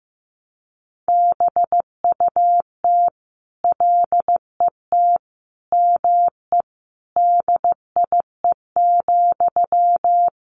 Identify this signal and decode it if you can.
Morse code, heard as a keyed tone, about 15 wpm, 700 Hz. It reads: BUT LET ME DIE,
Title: Sentences from Top 1300 Words 15wpm